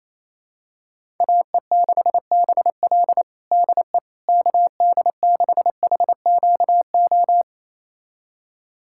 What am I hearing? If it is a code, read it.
Morse code, heard as a keyed tone, about 28 wpm, 700 Hz. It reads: AE6BL DE KD6HQO
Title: USA Callsign DE Callsign 28wpm